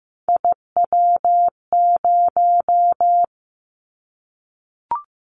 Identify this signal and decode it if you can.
Morse code, heard as a keyed tone, about 15 wpm, 700 Hz. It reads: IW0
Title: Callsign Prefixes for Top 25 DXCC - 15wpm